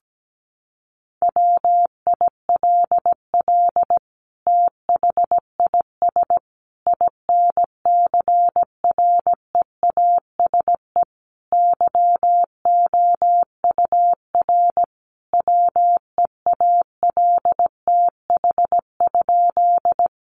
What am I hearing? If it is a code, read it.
Morse code, heard as a keyed tone, about 17 wpm, 700 Hz. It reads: WILL THIS INCREASE YOUR WEALTH?
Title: Sentences from Top 1200 Words 17wpm